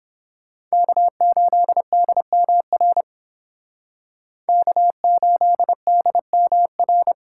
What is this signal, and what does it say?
Morse code, heard as a keyed tone, about 29 wpm, 700 Hz. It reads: K8DMR K8DMR